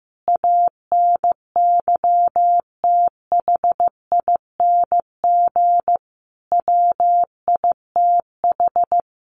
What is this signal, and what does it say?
Morse code, heard as a keyed tone, about 14 wpm, 700 Hz. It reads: ANYTHING WITH
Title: Sentences from Top 1000 Words 15wpm